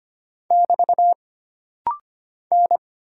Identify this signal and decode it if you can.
Morse code, heard as a keyed tone, about 25 wpm, 700 Hz. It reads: <BT> N
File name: Single Character (Letter:Number:Punctuation:Prosign) - Rapid-Fire - 25wpm